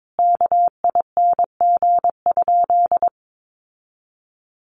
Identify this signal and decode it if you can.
Morse code, heard as a keyed tone, about 22 wpm, 700 Hz. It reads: KING?
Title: Sentences from Top 1400 Words 22wpm